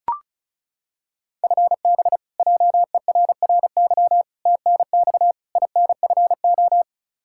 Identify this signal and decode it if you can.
Morse code, heard as a keyed tone, about 36 wpm, 700 Hz. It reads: FB JERRY TNX INFO
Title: K1USN SST QSO Practice 35wpm